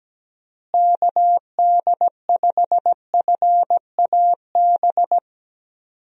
Morse code, heard as a keyed tone, about 17 wpm, 700 Hz. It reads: KD5FAB
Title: USA Call Signs - 6 Characters - Repeated 2x - 17wpm